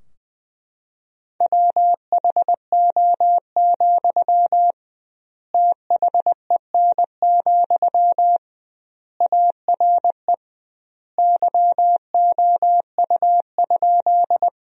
Morse code, heard as a keyed tone, about 21 wpm, 700 Hz. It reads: WHO, THEN, ARE YOU?
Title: 2x Word-Spacing - Sentences from Top 100 Words - 20wpm